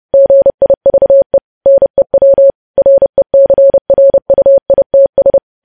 Morse code, heard as a keyed tone, about 30 wpm, 550 Hz. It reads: GIVE NEW RECRUITS
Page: news headlines in morse code at 30 wpm